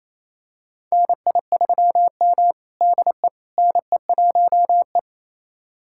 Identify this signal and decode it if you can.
Morse code, heard as a keyed tone, about 28 wpm, 700 Hz. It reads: NI3M DE NE1E
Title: USA Callsign DE Callsign - 4 Characters - 28wpm